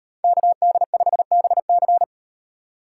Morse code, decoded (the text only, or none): KD5BC